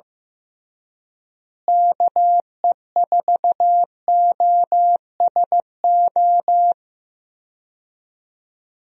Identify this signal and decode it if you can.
Morse code, heard as a keyed tone, about 15 wpm, 700 Hz. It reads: KE4OSO
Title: USA Call Signs - 6 Characters - Repeated 3x - 15wpm